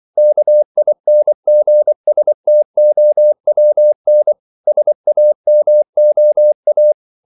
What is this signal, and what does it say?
Morse code, heard as a keyed tone, about 24 wpm, 600 Hz. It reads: KINGSTOWN SAMOA
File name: CW / Practice / NationalCapitals24